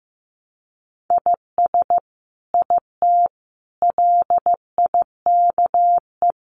Morse code, heard as a keyed tone, about 15 wpm, 700 Hz. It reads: IS IT LIKE
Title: Sentences from Top 100 Words 15wpm